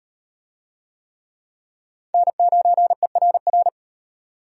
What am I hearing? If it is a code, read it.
Morse code, heard as a keyed tone, about 36 wpm, 700 Hz. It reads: N9ERR